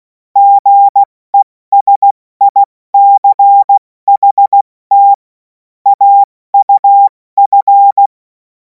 Morse code, heard as a keyed tone, about 16 wpm, spaced as for 14 wpm, 800 Hz. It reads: GESICHT AUF
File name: Morsekurs / Witze / Witz 11. August 2026 70 F80